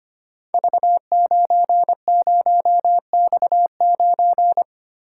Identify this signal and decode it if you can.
Morse code, heard as a keyed tone, about 25 wpm, 700 Hz. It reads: V90X9
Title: Code Groups - 5 Letter:Numbers - 25wpm